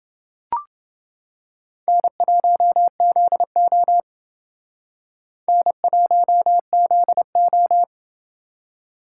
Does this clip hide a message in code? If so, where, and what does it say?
Yes: Morse code from 1.9 to 7.8 s, heard as a keyed tone, about 29 wpm, 700 Hz. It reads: N1ZO N1ZO